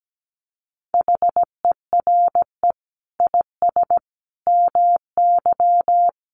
Morse code, heard as a keyed tone, about 17 wpm, 700 Hz. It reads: HERE IS MY
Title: Sentences from Top 1400 Words 17wpm